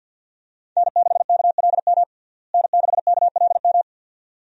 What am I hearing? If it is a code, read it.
Morse code, heard as a keyed tone, about 52 wpm, 700 Hz. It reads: N6KLM N6KLM
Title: USA Call Signs - 5 Characters - Repeated 2x - 50wpm